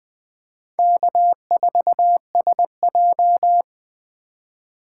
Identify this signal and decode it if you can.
Morse code, heard as a keyed tone, about 23 wpm, 700 Hz. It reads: K4SJ
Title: Speed Racing - USA 1x2 Call Signs 20wpm